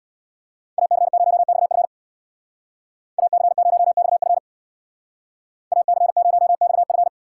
Morse code, heard as a keyed tone, about 65 wpm, 700 Hz. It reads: AC0XF AC0XF AC0XF